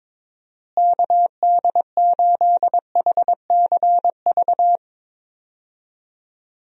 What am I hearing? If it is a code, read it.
Morse code, heard as a keyed tone, about 22 wpm, 700 Hz. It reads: KD8HCV
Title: USA Call Signs - 6 Characters - Repeated 2x - 22wpm